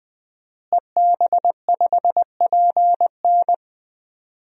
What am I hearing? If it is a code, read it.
Morse code, heard as a keyed tone, about 20 wpm, 700 Hz. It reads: EB5PN
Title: DXCC-100 Call Signs - 5 Characters - Repeated 2x - 20wpm